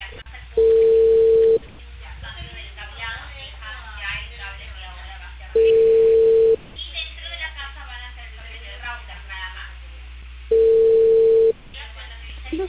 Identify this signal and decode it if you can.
Ringback tone (China, 450 Hz), with speech.